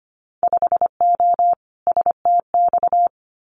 Morse code, heard as a keyed tone, about 25 wpm, 700 Hz. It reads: 5O STX
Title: Winter Field Day Exchanges 25wpm